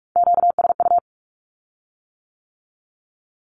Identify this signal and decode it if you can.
Morse code, heard as a keyed tone, about 46 wpm, 700 Hz. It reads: QSU
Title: Q-Codes with Spoken Meaning - 45wpm